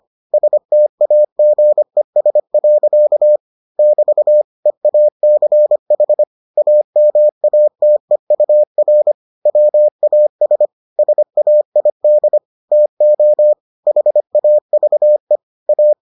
Morse code, heard as a keyed tone, about 25 wpm, 600 Hz. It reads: STAGES. <BT> EACH AMATEUR WAS SAID TO HAVE A